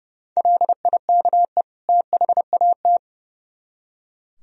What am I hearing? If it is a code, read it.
Morse code, heard as a keyed tone, about 36 wpm, 700 Hz. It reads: LIKE THAT
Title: Speed-Racing - Sets of 2 Words 30wpm